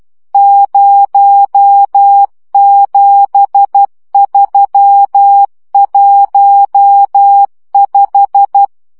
Morse code, heard as a keyed tone, about 12 wpm, 800 Hz. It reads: 07315
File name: morze vizsga 2011-3 10